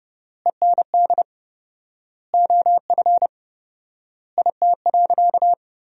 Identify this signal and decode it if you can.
Morse code, heard as a keyed tone, about 30 wpm, 700 Hz. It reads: END OF IT.
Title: 4x Word-Spacing - Sentences from Top 200 Words - 30wpm